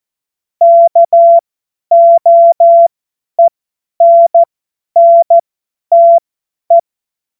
Morse code, heard as a keyed tone, about 14 wpm, spaced as for 10 wpm, 680 Hz. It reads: KOENNTE
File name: Morsekurs / Witze / Witz 24 Juni 2025 50 F70